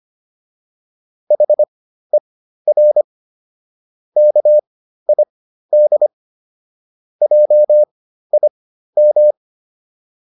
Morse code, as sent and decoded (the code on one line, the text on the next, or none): .... .-. / -.- .. -.. / .--- .. --
HER KID JIM